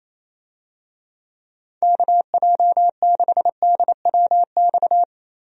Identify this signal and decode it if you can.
Morse code, heard as a keyed tone, about 28 wpm, 700 Hz. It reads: KJ6DWX
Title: USA Call Signs - 6 Characters - Repeated 3x - 28wpm